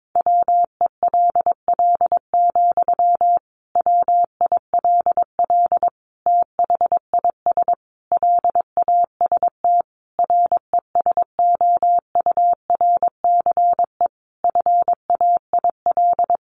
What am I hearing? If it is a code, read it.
Morse code, heard as a keyed tone, about 22 wpm, 700 Hz. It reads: WELL, WILL THIS LAST RESOURCE FAIL
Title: Sentences from Top 1300 Words 22wpm